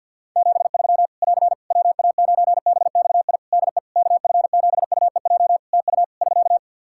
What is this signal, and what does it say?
Morse code, heard as a keyed tone, about 49 wpm, 700 Hz. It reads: 73 <AR> WA9BXI DE KF7UEJ TU <SK>
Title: K1USN SST QSO Practice 50wpm